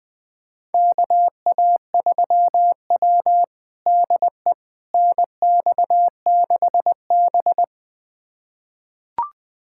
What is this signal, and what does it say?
Morse code, heard as a keyed tone, about 20 wpm, 700 Hz. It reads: KA3W DE NX6B
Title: USA Callsign DE Callsign - 4 Characters - 20wpm